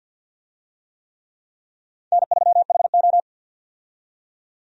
Morse code, heard as a keyed tone, about 50 wpm, 700 Hz. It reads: N3HO
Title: USA Call Signs - 4 Characters - 50wpm